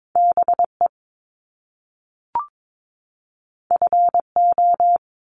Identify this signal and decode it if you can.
Morse code, heard as a keyed tone, about 22 wpm, 700 Hz. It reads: BE FO
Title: Top Two Letter Combinations in Words 22wpm